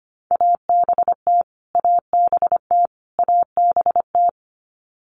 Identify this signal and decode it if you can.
Morse code, heard as a keyed tone, about 25 wpm, 700 Hz. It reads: ABT ABT ABT
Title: Abbreviations Repeated 3x - 25wpm